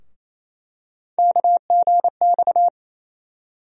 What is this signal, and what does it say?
Morse code, heard as a keyed tone, about 28 wpm, 700 Hz. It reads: KGX
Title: USA Call Sign Postfixes - 3 Letters 28wpm